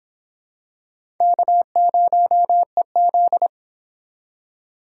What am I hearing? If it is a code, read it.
Morse code, heard as a keyed tone, about 25 wpm, 700 Hz. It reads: K0EZ